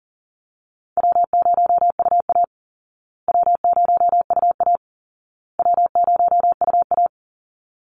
Morse code, heard as a keyed tone, about 40 wpm, 700 Hz. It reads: W0UA W0UA W0UA
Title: Easy Call Signs Repeated 3x - 40wpm